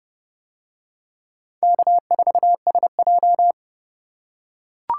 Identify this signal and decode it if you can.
Morse code, heard as a keyed tone, about 28 wpm, 700 Hz. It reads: K4SJ